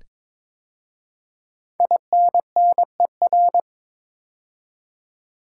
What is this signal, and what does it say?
Morse code, heard as a keyed tone, about 22 wpm, 700 Hz. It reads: INNER
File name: 5 Letter Words - First 3 Letters Given - 22wpm